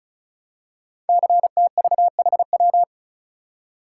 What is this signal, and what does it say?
Morse code, heard as a keyed tone, about 36 wpm, 690 Hz. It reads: CTVHW
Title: Code Groups - 5 Letters - 35wpm